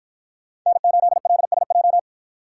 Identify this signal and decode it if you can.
Morse code, heard as a keyed tone, about 53 wpm, 700 Hz. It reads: N8LSJ